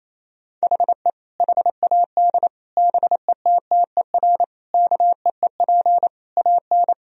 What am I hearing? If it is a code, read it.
Morse code, heard as a keyed tone, about 28 wpm, 700 Hz. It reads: HE HAD BETTER KEEP AN
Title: Sentences from Top 1000 Words - Review Part 1 - 28wpm